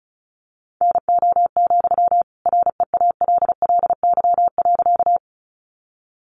Morse code, heard as a keyed tone, about 34 wpm, 700 Hz. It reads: NO, REALLY.
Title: Sentences from Top 600 Words 35wpm